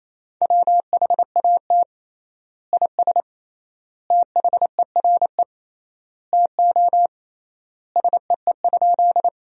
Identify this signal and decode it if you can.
Morse code, heard as a keyed tone, about 28 wpm, 700 Hz. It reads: WHAT IS THERE TO SEE?
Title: 3x Word-Spacing - Sentences from Top 100 Words - 28wpm